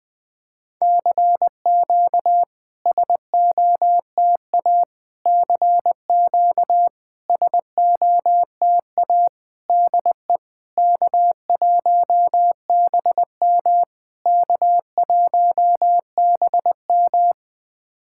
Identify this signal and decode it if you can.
Morse code, heard as a keyed tone, about 20 wpm, 700 Hz. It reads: CQ SOTA CQ SOTA DE K1BM K1BM
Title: SOTA QSO Practice 20wpm